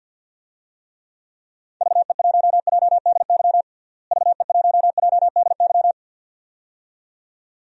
Morse code, heard as a keyed tone, about 52 wpm, 700 Hz. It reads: VE1JDY VE1JDY
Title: DXCC-100 Call Signs - 6 Characters - Repeated 2x - 50wpm